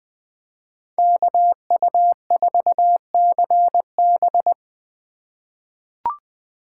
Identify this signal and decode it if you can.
Morse code, heard as a keyed tone, about 20 wpm, 700 Hz. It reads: KU4CB
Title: USA Call Signs - 5 Characters - 3x Character Spacing - 20wpm